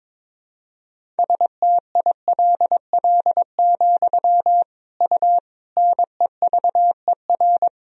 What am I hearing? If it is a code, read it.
Morse code, heard as a keyed tone, about 22 wpm, 700 Hz. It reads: STILL, U NEVER
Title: Sentences from Top 300 Words and Abbreviations 22wpm